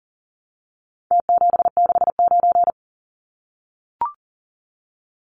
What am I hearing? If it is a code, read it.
Morse code, heard as a keyed tone, about 40 wpm, 700 Hz. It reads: T769
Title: Numbers - Contest Style - 4-Digit 40wpm